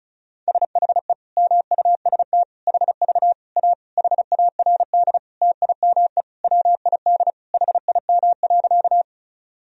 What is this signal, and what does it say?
Morse code, heard as a keyed tone, about 36 wpm, 700 Hz. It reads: SHE MUST HV A HARD TIME WID HIM.
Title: Sentences from Top 300 Words and Abbreviations 35wpm